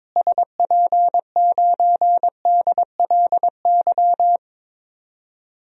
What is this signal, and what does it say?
Morse code, heard as a keyed tone, about 22 wpm, 700 Hz. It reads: SP9DLY
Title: DXCC-100 Call Signs - 6 Characters - Repeated 3x - 22wpm